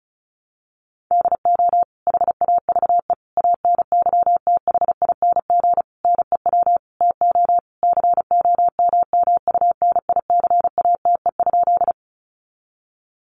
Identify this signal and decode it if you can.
Morse code, heard as a keyed tone, about 34 wpm, 700 Hz. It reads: DO HAVE ANYTHING NEW TO COMMUNICATE?
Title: Sentences from Top 1300 Words 35wpm